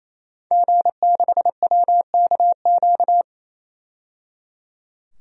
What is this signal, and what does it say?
Morse code, heard as a keyed tone, about 28 wpm, 700 Hz. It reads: G6WKQ
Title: DXCC-100 Call Signs - 5 Characters - 28wpm